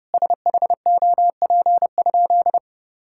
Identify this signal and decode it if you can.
Morse code, heard as a keyed tone, about 32 wpm, 700 Hz. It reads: SHOP?
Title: Sentences from Top 1600 Words 30wpm